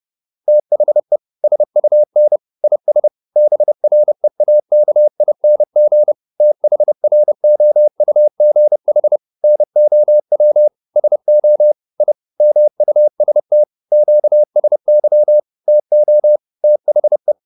Morse code, heard as a keyed tone, about 30 wpm, 600 Hz. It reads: THE SUN IS BREAKING THROUGH NOW SO I MUST QSY TO THE